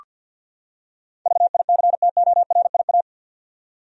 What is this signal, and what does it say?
Morse code, heard as a keyed tone, about 48 wpm, 700 Hz. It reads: VICTORIA
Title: USA Common Names - 50wpm